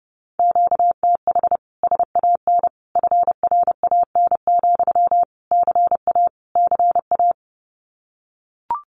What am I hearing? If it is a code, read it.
Morse code, heard as a keyed tone, about 30 wpm, 700 Hz. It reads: QTH SAN FRAN, CA CA
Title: Common QSO Phrases Encore 30wpm